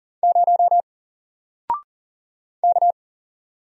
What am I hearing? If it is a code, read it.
Morse code, heard as a keyed tone, about 40 wpm, 700 Hz. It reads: 0 K
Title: Single Character (Letter:Number:Punctuation:Prosign) 40wpm